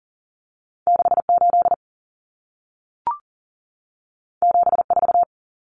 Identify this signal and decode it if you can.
Morse code, heard as a keyed tone, about 40 wpm, 700 Hz. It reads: 68 74
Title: Numbers - 2-Digit 40wpm